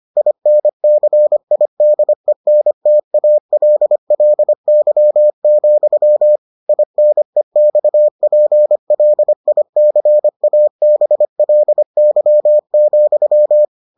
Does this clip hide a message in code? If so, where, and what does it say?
Yes: Morse code from 0.2 to 13.6 s, heard as a keyed tone, about 25 wpm, 600 Hz. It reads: INCIDENTALLY, INEXPLICABLY,